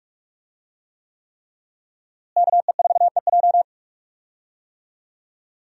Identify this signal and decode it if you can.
Morse code, heard as a keyed tone, about 46 wpm, 700 Hz. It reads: KE4EJ
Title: USA Call Signs - 5 Characters - Repeated 3x - 45wpm